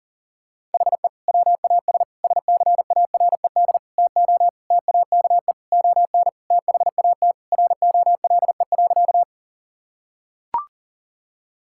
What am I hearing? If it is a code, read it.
Morse code, heard as a keyed tone, about 40 wpm, 700 Hz. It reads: HE WAS SCARED TO TAKE ON THAT ROLE.
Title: Sentences from Top 1000 Words - Review Part 6 - 40wpm